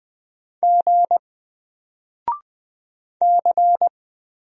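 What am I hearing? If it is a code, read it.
Morse code, heard as a keyed tone, about 20 wpm, 700 Hz. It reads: G C
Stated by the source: Single letter and number Morse drill at 20 wpm, from N4LCD.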